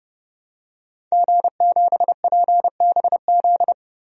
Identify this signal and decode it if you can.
Morse code, heard as a keyed tone, about 30 wpm, 700 Hz. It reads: G7PBZ
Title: DXCC-100 Call Signs - 5 Characters - Repeated 3x - 30wpm